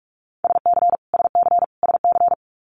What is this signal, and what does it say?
Morse code, heard as a keyed tone, about 44 wpm, 700 Hz. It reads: SC SC SC
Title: US States Repeated 3x 45wpm